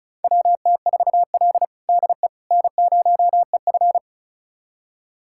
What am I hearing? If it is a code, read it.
Morse code, heard as a keyed tone, about 34 wpm, 700 Hz. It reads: WT4L DE N0EF
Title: USA Callsign DE Callsign - 4 Characters - 35wpm